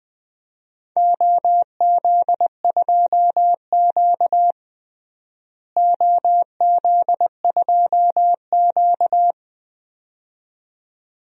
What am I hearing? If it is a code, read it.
Morse code, heard as a keyed tone, about 20 wpm, 700 Hz. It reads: OZ2Q OZ2Q
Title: DXCC-100 Call Signs - 4 Characters - Repeated 2x - 20wpm